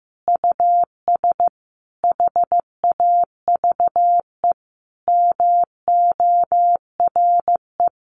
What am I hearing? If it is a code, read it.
Morse code, heard as a keyed tone, about 15 wpm, 700 Hz. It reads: US HAVE MORE
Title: Sentences from Top 1300 Words 15wpm